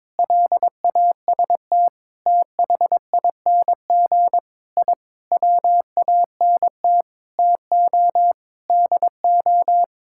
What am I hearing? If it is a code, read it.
Morse code, heard as a keyed tone, about 22 wpm, 700 Hz. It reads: LAST THING I WANT TO DO
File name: practice / 22wpm / Sets of 7 Words - 22wpm